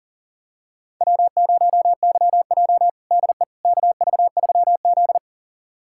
Morse code, heard as a keyed tone, about 40 wpm, 700 Hz. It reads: W0YJ DE KV3Z
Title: USA Callsign DE Callsign - 4 Characters - 40wpm